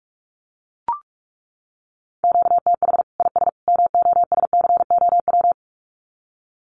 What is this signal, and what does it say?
Morse code, heard as a keyed tone, about 44 wpm, 700 Hz. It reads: QTH IS MOSCOW
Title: Common QSO Phrases 45wpm